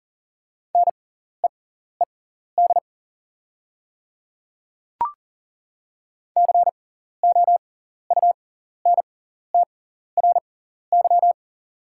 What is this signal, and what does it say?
Morse code, heard as a keyed tone, about 40 wpm, spaced as for 14 wpm, 690 Hz. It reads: NEED COUNTRY
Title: 6x Character Spacing - Top 500 Words - 40wpm